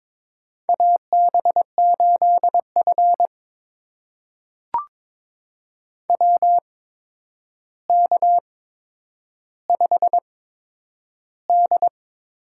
Morse code, heard as a keyed tone, about 22 wpm, spaced as for 9 wpm, 700 Hz. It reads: AB8F WK5D
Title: USA Call Signs - 4 Characters - 8x Character Spacing - 22wpm